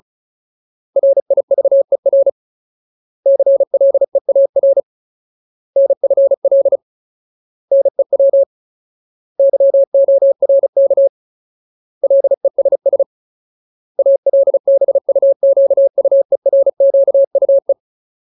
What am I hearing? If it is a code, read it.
Morse code, heard as a keyed tone, about 34 wpm, 550 Hz. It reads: RIVER CLEAR NFL NEW YORK LESS ALBUQUERQUE
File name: Ultimate Word List / 35WPM Ultimate